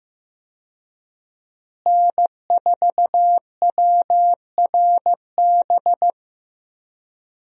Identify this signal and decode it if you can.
Morse code, heard as a keyed tone, about 15 wpm, 700 Hz. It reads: N4WRB
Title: USA Call Signs - 5 Characters - Repeated 3x - 15wpm